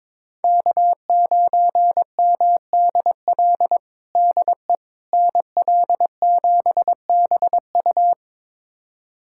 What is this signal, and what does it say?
Morse code, heard as a keyed tone, about 21 wpm, 700 Hz. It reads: K9MDL DE NL7BU
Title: USA Callsign DE Callsign - 5 Characters - 22wpm